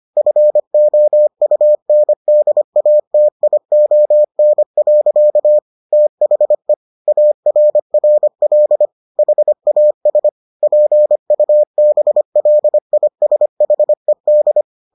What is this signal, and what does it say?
Morse code, heard as a keyed tone, about 25 wpm, 600 Hz. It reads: FOUNDATION. THE ARRL HAS PUBLISHED